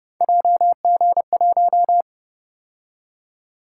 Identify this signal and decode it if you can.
Morse code, heard as a keyed tone, about 29 wpm, 700 Hz. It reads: JG1
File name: Speed-Racing - Three Letter-Number Combinations - 25wpm